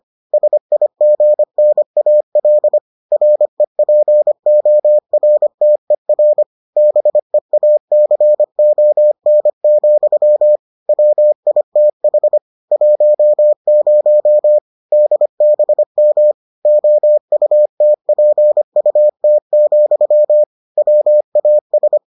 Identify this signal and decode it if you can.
Morse code, heard as a keyed tone, about 25 wpm, 600 Hz. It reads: SIGNAL REPORTER BEACON, WITH 10 DBM OUTPUT, WAS